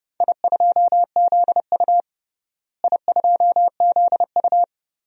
Morse code, heard as a keyed tone, about 30 wpm, 700 Hz. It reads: I2ZU I2ZU
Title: DXCC-100 Call Signs - 4 Characters - Repeated 3x - 30wpm